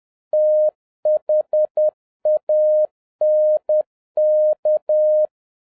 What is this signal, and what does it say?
Morse code, heard as a keyed tone, about 10 wpm, 610 Hz. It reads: THANK